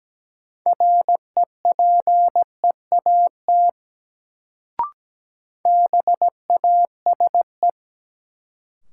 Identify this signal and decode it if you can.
Morse code, heard as a keyed tone, about 17 wpm, 700 Hz. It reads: REPEAT BASE
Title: Sentences from Top 1600 Words - New Word Drill Alt - 17wpm